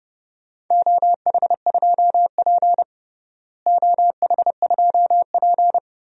Morse code, heard as a keyed tone, about 30 wpm, 700 Hz. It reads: OH2P OH2P
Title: DXCC-100 Call Signs - 4 Characters - Repeated 3x - 30wpm